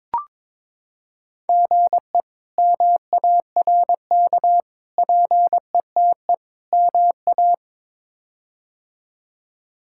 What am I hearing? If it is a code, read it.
Morse code, heard as a keyed tone, about 22 wpm, 700 Hz. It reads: GE MARK PETE MA
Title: K1USN SST QSO Practice - 22wpm